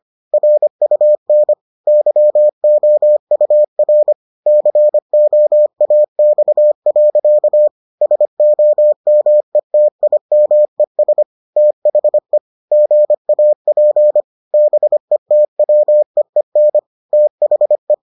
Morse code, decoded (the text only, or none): RUN YOUR COAX. SOMETIMES THE GAP BETWEEN THE